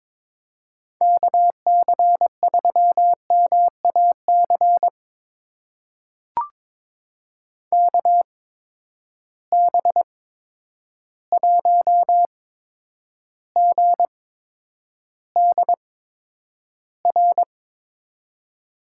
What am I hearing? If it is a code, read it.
Morse code, heard as a keyed tone, about 22 wpm, spaced as for 10 wpm, 700 Hz. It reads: KC3MAC KB1GDR
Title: USA Call Signs - 8x Character Spacing - 22wpm